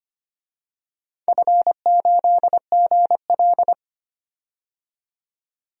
Morse code, heard as a keyed tone, about 25 wpm, 700 Hz. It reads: F8GL
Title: DXCC-100 Call Signs - 4 Characters - 25wpm